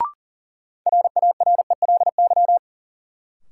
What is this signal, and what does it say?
Morse code, heard as a keyed tone, about 40 wpm, 700 Hz. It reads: RARELY
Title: Sentences from Top 1400 Words - New Word Drill - 40wpm